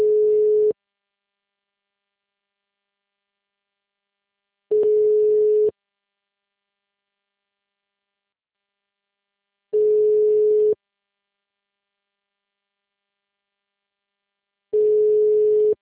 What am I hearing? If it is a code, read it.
Ringback tone (Europe, 425 Hz).